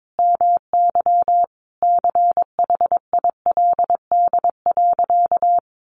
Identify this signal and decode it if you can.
Morse code, heard as a keyed tone, about 22 wpm, 700 Hz. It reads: MY CHILD.
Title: Sentences from Top 1300 Words 22wpm